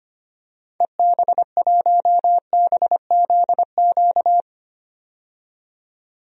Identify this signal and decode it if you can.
Morse code, heard as a keyed tone, about 25 wpm, 700 Hz. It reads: EB1BZQ
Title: DXCC-100 Call Signs - 6 Characters - 25wpm